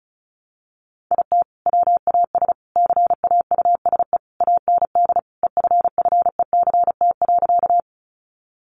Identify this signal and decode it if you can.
Morse code, heard as a keyed tone, about 34 wpm, 700 Hz. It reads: IT WAS CAUSE AND EFFECT.
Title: Sentences from Top 1000 Words 35wpm